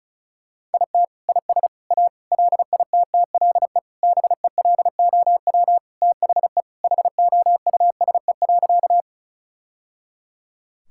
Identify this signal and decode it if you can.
Morse code, heard as a keyed tone, about 36 wpm, 700 Hz. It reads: IT IS A LITTLE BELOW THE HOUSE.
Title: Sentences from Top 1000 Words - Review Part 5 - 35wpm